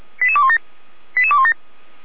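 Five-tone selective call (ZVEI), heard as quick runs of five tones, tones of about 70 ms. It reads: 88217 88217
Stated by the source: ZVEI five-tone call 88399 received on an SDR, from the Feuerwehr Orlamünde audio server.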